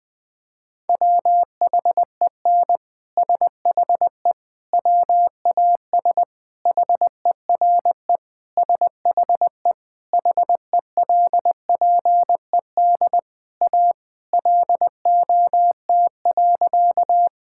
Morse code, heard as a keyed tone, about 20 wpm, 700 Hz. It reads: WHEN SHE WAS HERE SHE HELPED A LOT.
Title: Sentences with 3-Word Phrases - 20wpm